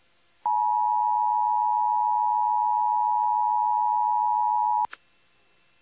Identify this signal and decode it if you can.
Emergency Alert System (SAME) tones (the 853 and 960 Hz attention tone). It is heard from 0.4 to 4.9 s.